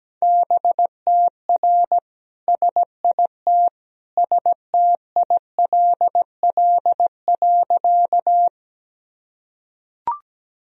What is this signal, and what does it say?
Morse code, heard as a keyed tone, about 17 wpm, 700 Hz. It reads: BTR SIT STILL.
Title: Sentences from Top 500 Words and Abbreviations 17wpm